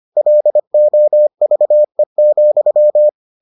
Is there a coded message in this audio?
Yes: Morse code from 0.2 to 3.1 s, heard as a keyed tone, about 25 wpm, 600 Hz. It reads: LOVE,